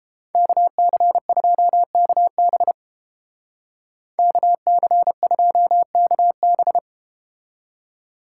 Morse code, heard as a keyed tone, about 30 wpm, 700 Hz. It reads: KC2KB KC2KB